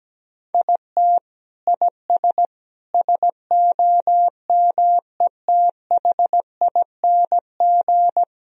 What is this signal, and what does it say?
Morse code, heard as a keyed tone, about 17 wpm, 700 Hz. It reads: IT IS SOMETHING
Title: Sentences from Top 1000 Words - Review Part 3 - 17wpm